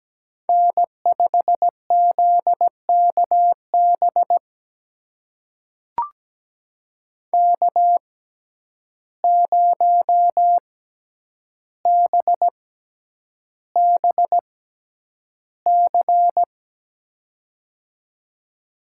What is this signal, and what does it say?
Morse code, heard as a keyed tone, about 17 wpm, spaced as for 9 wpm, 700 Hz. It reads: N5ZKB K0BBC
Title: USA Call Signs - 5 Characters - 6x Character Spacing - 17wpm